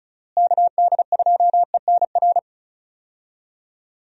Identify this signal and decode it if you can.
Morse code, heard as a keyed tone, about 34 wpm, 700 Hz. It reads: KD2ENR
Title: USA Call Signs - 6 Characters - Repeated 2x - 35wpm